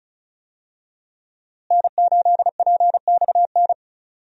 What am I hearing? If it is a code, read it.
Morse code, heard as a keyed tone, about 36 wpm, 700 Hz. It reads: N8PXN
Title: USA Call Signs - 5 Characters - Repeated 3x - 35wpm